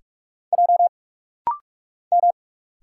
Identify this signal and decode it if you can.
Morse code, heard as a keyed tone, about 44 wpm, 700 Hz. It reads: J M
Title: Single Letter-Number Alt - Rapid-Fire 45wpm